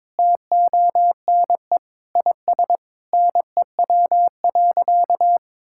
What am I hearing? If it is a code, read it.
Morse code, heard as a keyed tone, about 22 wpm, 700 Hz. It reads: TONE IS NEW.
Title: Sentences from Top 1600 Words 22wpm